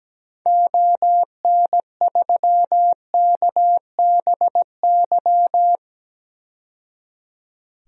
Morse code, heard as a keyed tone, about 17 wpm, 700 Hz. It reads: ON3KBY
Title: DXCC-100 Call Signs - 6 Characters - 17wpm